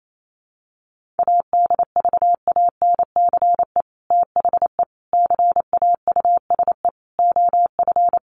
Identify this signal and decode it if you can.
Morse code, heard as a keyed tone, about 28 wpm, 700 Hz. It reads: ADVANCE THE CAUSE OF
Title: Sentences from Top 1400 Words 28wpm